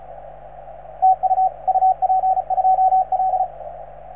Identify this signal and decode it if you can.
Morse code, heard as a keyed tone, about 34 wpm, 750 Hz, with radio noise. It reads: TU UP2L